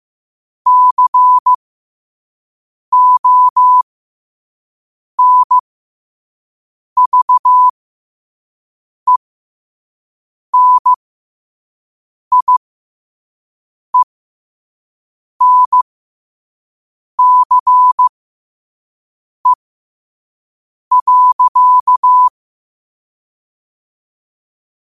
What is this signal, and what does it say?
Morse code, heard as a keyed tone, about 15 wpm, spaced as for 5 wpm, 1000 Hz. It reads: CONVENIENCE.
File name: qotdfiles / 5wpm